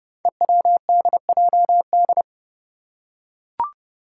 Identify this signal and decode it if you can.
Morse code, heard as a keyed tone, about 30 wpm, 700 Hz. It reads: EWDJD
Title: Five Challenging Letter-Number Combinations 30wpm